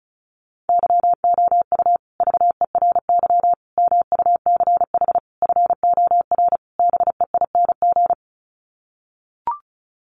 Morse code, heard as a keyed tone, about 34 wpm, 700 Hz. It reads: YOU VERY MUCH FOR BEING
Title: Sets of 5 Words Encore 35wpm